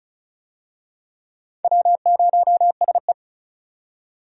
Morse code, heard as a keyed tone, about 36 wpm, 700 Hz. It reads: W0SE
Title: USA Call Signs - 4 Characters - 35wpm